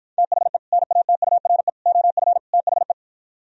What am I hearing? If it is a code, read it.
Morse code, heard as a keyed tone, about 53 wpm, 700 Hz. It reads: THE NATURE OF THE